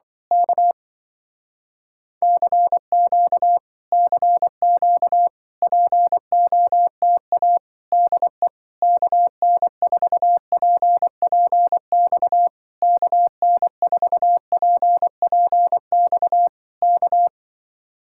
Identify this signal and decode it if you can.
Morse code, heard as a keyed tone, about 25 wpm, 700 Hz. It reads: K CQ CQ POTA DE KN4PPX KN4PPX K